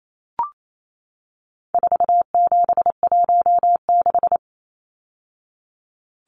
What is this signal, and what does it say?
Morse code, heard as a keyed tone, about 28 wpm, 700 Hz. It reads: 4716
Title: Numbers - 4-Digit 28wpm